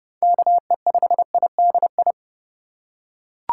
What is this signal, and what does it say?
Morse code, heard as a keyed tone, about 31 wpm, 700 Hz. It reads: KE5IDI
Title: USA Call Signs - 2x Character Spacing - 30wpm